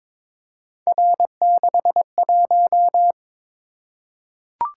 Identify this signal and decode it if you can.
Morse code, heard as a keyed tone, about 22 wpm, 700 Hz. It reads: R61
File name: practice / 22wpm / Three Letter-Number Combinations - 22wpm